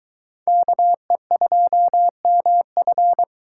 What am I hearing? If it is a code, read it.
Morse code, heard as a keyed tone, about 23 wpm, 700 Hz. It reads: KE2MF